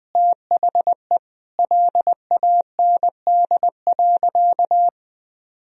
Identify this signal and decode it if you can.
Morse code, heard as a keyed tone, about 20 wpm, 700 Hz. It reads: THE LAND.
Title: Sentences from Top 1000 Words - Review Part 2 - 20wpm